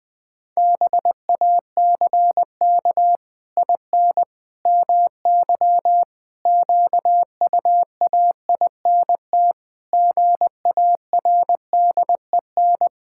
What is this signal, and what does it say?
Morse code, heard as a keyed tone, about 19 wpm, 700 Hz. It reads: BACK IN MY QUAINT GARDEN
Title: Pangram Challenge v2 - 20wpm